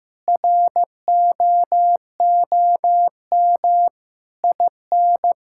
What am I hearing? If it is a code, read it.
Morse code, heard as a keyed tone, about 16 wpm, 700 Hz. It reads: ROOM IN
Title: Sentences from Top 1000 Words - Review Part 5 - 15wpm